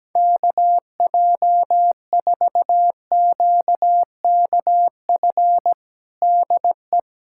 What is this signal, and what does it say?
Morse code, heard as a keyed tone, about 17 wpm, 700 Hz. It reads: KJ4QKF DE
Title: USA Callsign DE Callsign - 6 Characters - 17wpm